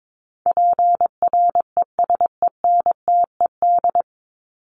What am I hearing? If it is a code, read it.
Morse code, heard as a keyed tone, about 22 wpm, 700 Hz. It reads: PRESENTED
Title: ED Suffix - Top 100 Words with Base Words 22wpm